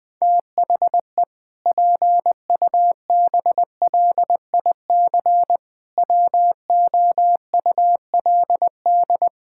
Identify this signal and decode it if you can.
Morse code, heard as a keyed tone, about 20 wpm, 700 Hz. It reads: THE PUBLIC WOULD
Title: Sentences from Top 1600 Words 20wpm